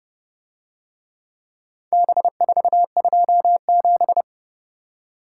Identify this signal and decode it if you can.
Morse code, heard as a keyed tone, about 30 wpm, 700 Hz. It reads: B427